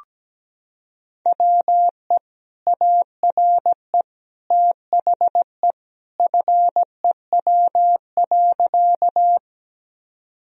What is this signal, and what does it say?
Morse code, heard as a keyed tone, about 17 wpm, 700 Hz. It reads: WE ARE THE FEW.
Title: Sentences from Top 1000 Words - Review Part 1 - 17wpm